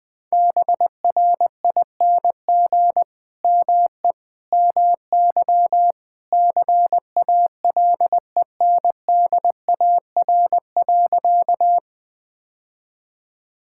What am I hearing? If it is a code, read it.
Morse code, heard as a keyed tone, about 21 wpm, 700 Hz. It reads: BRING ME MY CALENDAR.
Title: Sentences from Top 1600 Words 20wpm